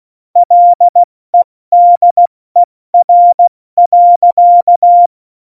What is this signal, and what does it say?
Morse code, heard as a keyed tone, about 16 wpm, 700 Hz. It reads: LEDER.
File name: Morsekurs / Witze / Witz 24. November 2020, 70 F80